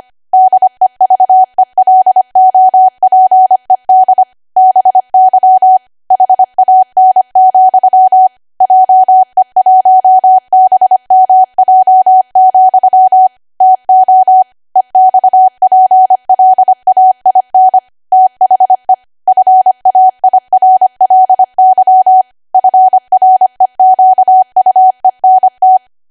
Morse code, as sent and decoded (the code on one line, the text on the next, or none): -.. ...- . .-.. --- .--. -.. / -... -.-- / .... .- -. --..-- / .--- . .---- -... -- .--- --..-- / - --- / . -..- .--. .-.. .- .. -. / - .... / ..-. .- .. .-. .-.. -.-- / ..-. .-. --.- ..- . -. -
DEVELOPED BY HAN, JE1BMJ, TO EXPLAIN THE FAIRLY FREQUENT